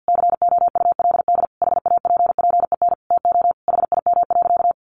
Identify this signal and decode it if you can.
Morse code, heard as a keyed tone, about 50 wpm, 700 Hz. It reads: COULD HAPPEN TO HIM.
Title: Sentences from Top 400 Words 50wpm